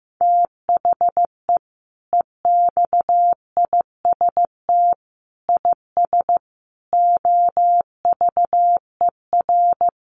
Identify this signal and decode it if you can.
Morse code, heard as a keyed tone, about 15 wpm, 700 Hz. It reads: THE EXIST IS OVER
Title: Sentences from Top 1100 Words 15wpm